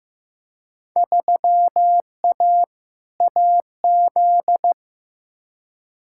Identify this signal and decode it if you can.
Morse code, heard as a keyed tone, about 15 wpm, 700 Hz. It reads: 3A AZ
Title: Field Day Exchanges - 15wpm